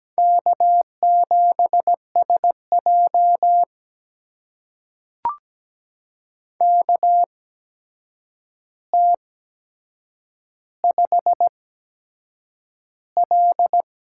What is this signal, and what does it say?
Morse code, heard as a keyed tone, about 17 wpm, spaced as for 7 wpm, 700 Hz. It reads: K7SJ KT5L
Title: USA Call Signs - 4 Characters - 8x Character Spacing - 17wpm